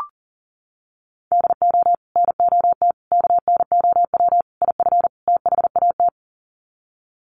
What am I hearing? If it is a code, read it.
Morse code, heard as a keyed tone, about 40 wpm, 700 Hz. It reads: DO NOT KNOW IF THAT
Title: Sets of 5 Words Encore2 40wpm